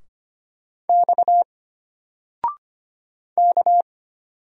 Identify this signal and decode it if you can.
Morse code, heard as a keyed tone, about 25 wpm, 700 Hz. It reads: X K
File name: Single-Letter-Number-25wpm